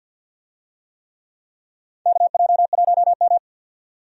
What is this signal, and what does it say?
Morse code, heard as a keyed tone, about 50 wpm, 700 Hz. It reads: KJ1M
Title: USA Call Signs - 4 Characters - 50wpm